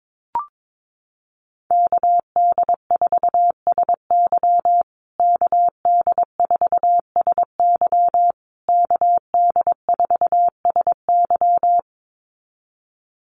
Morse code, heard as a keyed tone, about 22 wpm, 700 Hz. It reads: KD4SY KD4SY KD4SY
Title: USA Call Signs Repeated 3x 22wpm